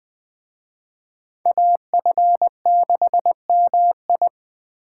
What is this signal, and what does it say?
Morse code, heard as a keyed tone, about 20 wpm, 700 Hz. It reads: AF6MI
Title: Speed Racing - USA 2x2 Call Signs 17wpm